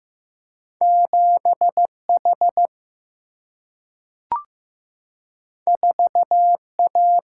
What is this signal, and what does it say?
Morse code, heard as a keyed tone, about 15 wpm, 700 Hz. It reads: 7H 4A
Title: Single-Letter Single-Number 15wpm